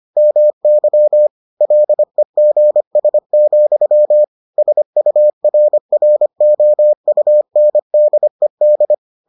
Morse code, heard as a keyed tone, about 25 wpm, 600 Hz. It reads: MY LEGS, SURROUNDED